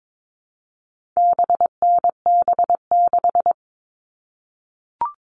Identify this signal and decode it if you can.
Morse code, heard as a keyed tone, about 22 wpm, 700 Hz. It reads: BNB6
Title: All Cut Numbers - 4-Digit 22wpm